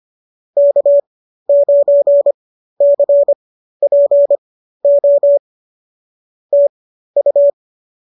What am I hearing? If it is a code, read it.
Morse code, heard as a keyed tone, about 25 wpm, spaced as for 13 wpm, 590 Hz. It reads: K9CPO TU